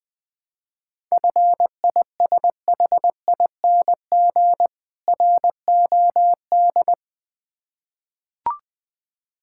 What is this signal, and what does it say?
Morse code, heard as a keyed tone, about 20 wpm, 690 Hz. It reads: FISHING ROD
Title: Difficult Words with Hints - 20wpm